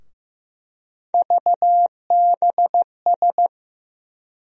Morse code, heard as a keyed tone, about 15 wpm, 700 Hz. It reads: VBS
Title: USA Call Sign Postfixes - 3 Letters 15wpm